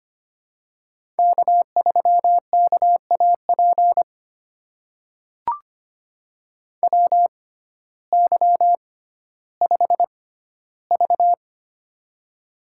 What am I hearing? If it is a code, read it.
Morse code, heard as a keyed tone, about 25 wpm, spaced as for 14 wpm, 700 Hz. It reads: K3KAP WY5V
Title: USA Call Signs - 6x Character Spacing - 25wpm